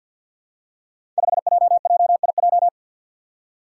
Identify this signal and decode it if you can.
Morse code, heard as a keyed tone, about 53 wpm, 700 Hz. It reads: 5JJIJ
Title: Five Challenging Letter-Number Combinations 50wpm